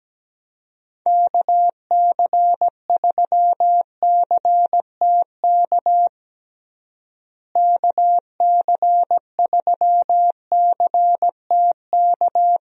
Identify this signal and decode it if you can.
Morse code, heard as a keyed tone, about 17 wpm, 700 Hz. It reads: KC3CTK KC3CTK
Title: USA Call Signs - 6 Characters - Repeated 3x - 17wpm